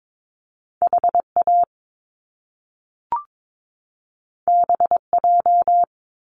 Morse code, heard as a keyed tone, about 22 wpm, 700 Hz. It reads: HA BJ
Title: Top Two Letter Combinations in Words 22wpm